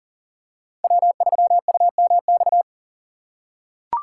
Morse code, heard as a keyed tone, about 38 wpm, 700 Hz. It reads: W3UMX
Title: USA Call Signs - 5 Characters - 4x Character Spacing - 40wpm